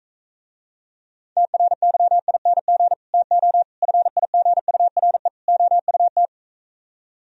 Morse code, heard as a keyed tone, about 45 wpm, 700 Hz. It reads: TRYING TO FIGURE OUT